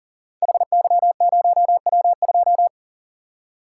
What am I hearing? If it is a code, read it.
Morse code, heard as a keyed tone, about 38 wpm, 700 Hz. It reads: HY0W2
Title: Code Groups - 5 Letter:Numbers - 40wpm